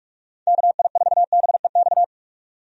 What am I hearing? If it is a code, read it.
Morse code, heard as a keyed tone, about 44 wpm, 700 Hz. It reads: KI4BEX
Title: USA Call Signs - 6 Characters - 45wpm